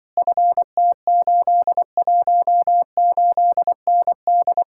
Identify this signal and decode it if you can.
Morse code, heard as a keyed tone, about 24 wpm, 700 Hz. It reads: FT818ND
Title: Speed-Racing Common QSO Phrases - 20wpm